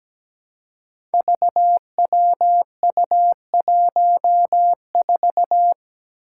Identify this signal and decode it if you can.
Morse code, heard as a keyed tone, about 17 wpm, 700 Hz. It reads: VWU14
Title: Five Challenging Letter-Number Combinations 17wpm